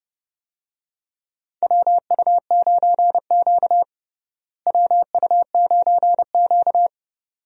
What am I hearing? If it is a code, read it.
Morse code, heard as a keyed tone, about 32 wpm, 700 Hz. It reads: WU9Q WU9Q
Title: USA Call Signs - 4 Characters - Repeated 2x - 30wpm